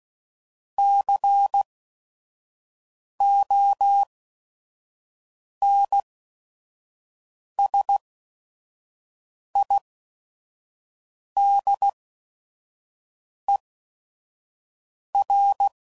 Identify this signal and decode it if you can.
Morse code, heard as a keyed tone, about 16 wpm, spaced as for 5 wpm, 800 Hz. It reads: CONSIDER
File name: CW Code Groups @5WPM / Technical Subjects / AND Gate